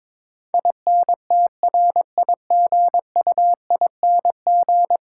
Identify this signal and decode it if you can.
Morse code, heard as a keyed tone, about 22 wpm, 700 Hz. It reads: INTRIGUING
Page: morse-code-ninja-practice-22wpm